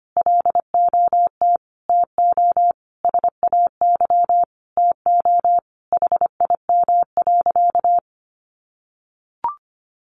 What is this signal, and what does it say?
Morse code, heard as a keyed tone, about 25 wpm, 700 Hz. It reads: LOT TO SAY TO HIM.
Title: Sentences from Top 1000 Words 25wpm